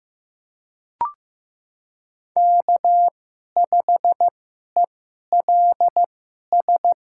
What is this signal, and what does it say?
Morse code, heard as a keyed tone, about 15 wpm, spaced as for 11 wpm, 700 Hz. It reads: K5ELS
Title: USA Call Signs - 5 Characters - 2x Character Spacing - 15wpm